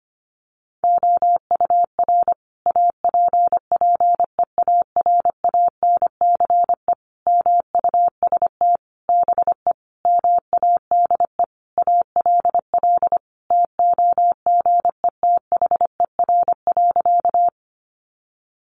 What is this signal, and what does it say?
Morse code, heard as a keyed tone, about 25 wpm, 700 Hz. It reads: OUR APPEARANCE MUST BE MADE ALL TOGETHER.
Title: Sentences from Top 1100 Words 25wpm